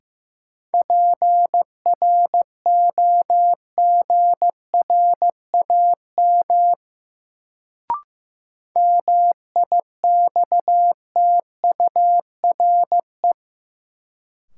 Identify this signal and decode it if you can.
Morse code, heard as a keyed tone, about 15 wpm, 700 Hz. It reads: PROGRAM MIXTURE
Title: Sentences from Top 1500 Words - New Word Drill Alt - 15wpm